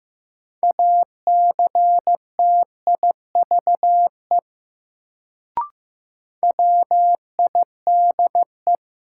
Morse code, heard as a keyed tone, about 15 wpm, 700 Hz. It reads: ACTIVE WIDE
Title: Sentences from Top 900 Words - New Word Drill - 15wpm